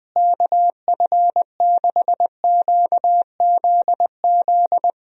Morse code, heard as a keyed tone, about 20 wpm, 700 Hz. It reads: KF6QZZ
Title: USA Call Signs - 6 Characters - Repeated 2x - 20wpm